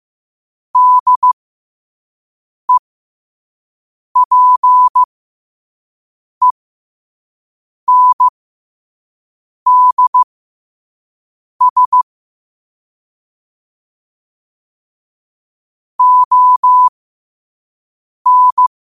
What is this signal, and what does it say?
Morse code, heard as a keyed tone, about 15 wpm, spaced as for 5 wpm, 1000 Hz. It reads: DEPENDS ON
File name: qotdfiles / 5wpm